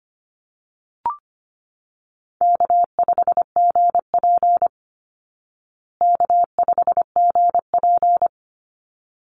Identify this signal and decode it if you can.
Morse code, heard as a keyed tone, about 25 wpm, 700 Hz. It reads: K5GP K5GP
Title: Call Signs Repeated 3x - 25wpm